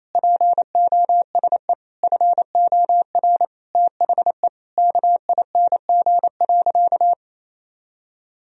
Morse code, heard as a keyed tone, about 28 wpm, 700 Hz. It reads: POSE FOR THE KING.
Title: Sentences from Top 1000 Words - Review Part 2 - 28wpm